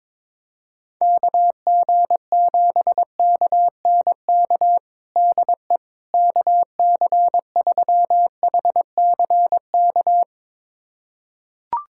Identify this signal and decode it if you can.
Morse code, heard as a keyed tone, about 22 wpm, 700 Hz. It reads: KG7KNK DE KC3HCK
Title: USA Callsign DE Callsign - 6 Characters - 22wpm